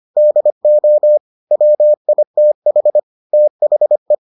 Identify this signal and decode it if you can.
Morse code, heard as a keyed tone, about 25 wpm, 600 Hz. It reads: DO WITH THE